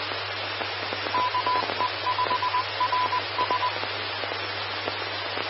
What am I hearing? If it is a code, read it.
Morse code, heard as a keyed tone, about 20 wpm, 1000 Hz, with radio noise. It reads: KE5RS
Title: Austin, TX Amateur Repeaters